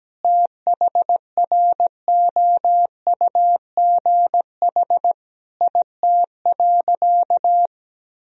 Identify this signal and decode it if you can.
Morse code, heard as a keyed tone, about 17 wpm, 700 Hz. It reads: THROUGH IT.
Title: Sentences from Top 1000 Words - Review Part 1 - 17wpm